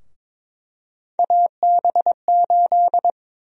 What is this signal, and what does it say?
Morse code, heard as a keyed tone, about 22 wpm, 700 Hz. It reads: AB8
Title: USA Call Sign Prefixes - Repeated 2x - 22wpm